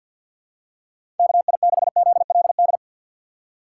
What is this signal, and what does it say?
Morse code, heard as a keyed tone, about 51 wpm, 700 Hz. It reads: KI6ZLD
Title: USA Call Signs - 6 Characters - 6x Character Spacing - 50wpm